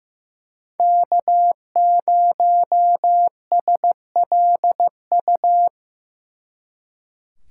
Morse code, heard as a keyed tone, about 18 wpm, 700 Hz. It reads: K0SLU
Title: Speed Racing - USA Call Signs 15wpm